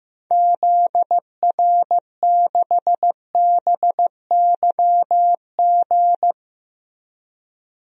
Morse code, heard as a keyed tone, about 16 wpm, 700 Hz. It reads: ZR6BYG
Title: DXCC-100 Call Signs - 6 Characters - Repeated 2x - 15wpm